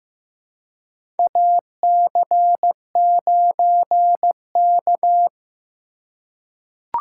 Morse code, heard as a keyed tone, about 15 wpm, 700 Hz. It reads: AC9K
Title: USA Call Signs - 3x Character Spacing - 15wpm